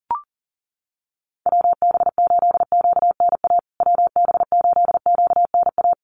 Morse code, heard as a keyed tone, about 40 wpm, 700 Hz. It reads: WB8QNA WB8QNA
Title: USA Call Signs Repeated 3x 40wpm